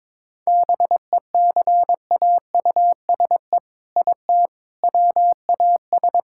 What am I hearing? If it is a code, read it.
Morse code, heard as a keyed tone, about 22 wpm, 700 Hz. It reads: BECAUSE IT WAS
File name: practice / 22wpm / Sets of 8 Words - 22wpm